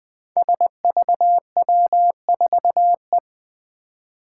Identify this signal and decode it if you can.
Morse code, heard as a keyed tone, about 20 wpm, 700 Hz. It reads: SVW4E
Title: Five Challenging Letter-Number Combinations 20wpm